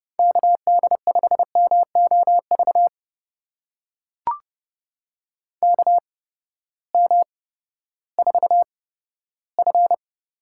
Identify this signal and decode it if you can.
Morse code, heard as a keyed tone, about 32 wpm, spaced as for 15 wpm, 700 Hz. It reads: KD5MOV KM4F
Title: USA Call Signs - 6 Characters - 8x Character Spacing - 30wpm